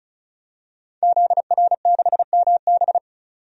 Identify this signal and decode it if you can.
Morse code, heard as a keyed tone, about 34 wpm, 700 Hz. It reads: ZR6MB
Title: DXCC-100 Call Signs - 5 Characters - 35wpm